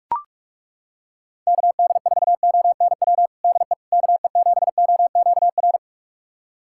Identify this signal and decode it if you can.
Morse code, heard as a keyed tone, about 42 wpm, 700 Hz. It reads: KD4ONW DE KE7OQR
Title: USA Callsign DE Callsign - 6 Characters - 45wpm